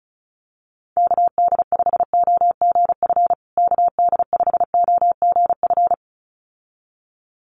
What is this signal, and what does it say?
Morse code, heard as a keyed tone, about 34 wpm, 700 Hz. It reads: KD5OGF KD5OGF
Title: USA Call Signs Repeated 2x 35wpm